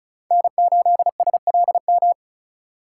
Morse code, heard as a keyed tone, about 34 wpm, 700 Hz. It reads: N8SLM
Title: USA Call Signs - 5 Characters - 2x Character Spacing - 35wpm